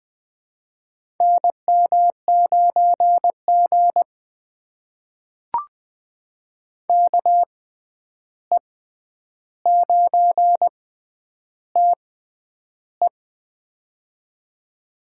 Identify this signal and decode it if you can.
Morse code, heard as a keyed tone, about 20 wpm, spaced as for 10 wpm, 700 Hz. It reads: NM9G KE9TE
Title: USA Call Signs - 6x Character Spacing - 20wpm